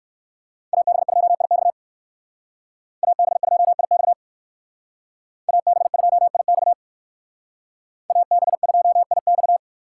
Morse code, heard as a keyed tone, about 62 wpm, 690 Hz. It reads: AB2IX AB2IX AB2IX AB2IX